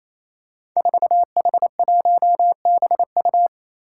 Morse code, heard as a keyed tone, about 28 wpm, 700 Hz. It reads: 4H1BU
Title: Five Challenging Letter-Number Combinations 28wpm